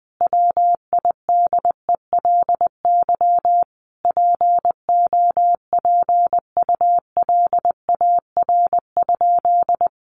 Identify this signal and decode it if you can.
Morse code, heard as a keyed tone, about 20 wpm, 700 Hz. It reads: WIDELY POPULAR?
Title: Sentences from Top 1200 Words 20wpm